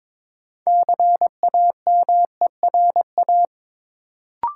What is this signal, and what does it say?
Morse code, heard as a keyed tone, about 22 wpm, 700 Hz. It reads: CAMERA
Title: Sentences from Top 1000 Words - Review Part 5 - New Word Drill - 22wpm